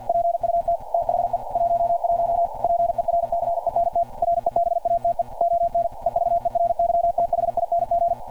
Morse code, heard as a keyed tone, about 50 wpm, 690 Hz. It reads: PUT THE 11 PERCENT REBATE ON EVERYTHING
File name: archived / CRSnet